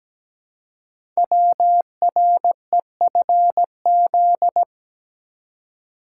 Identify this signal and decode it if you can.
Morse code, heard as a keyed tone, about 17 wpm, 700 Hz. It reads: WREFZ
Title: Code Groups - 5 Letter:Numbers - 17wpm